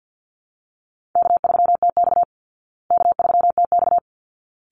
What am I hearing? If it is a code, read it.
Morse code, heard as a keyed tone, about 50 wpm, 700 Hz. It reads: K3TX K3TX
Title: Easy Call Signs Repeated 2x - 50wpm